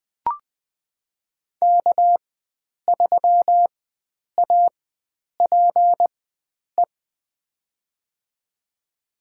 Morse code, heard as a keyed tone, about 20 wpm, spaced as for 9 wpm, 700 Hz. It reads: K3APE
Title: USA Call Signs - 5 Characters - 4x Character Spacing - 20wpm